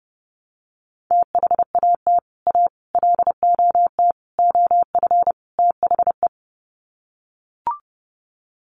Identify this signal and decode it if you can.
Morse code, heard as a keyed tone, about 28 wpm, 700 Hz. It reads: THAT A LOT OF THE
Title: Sets of 5 Words Encore2 30wpm